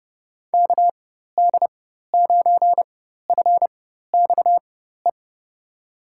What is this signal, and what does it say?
Morse code, heard as a keyed tone, about 30 wpm, spaced as for 14 wpm, 700 Hz. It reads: KD9FXE
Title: USA Call Signs - 6 Characters - 4x Character Spacing - 30wpm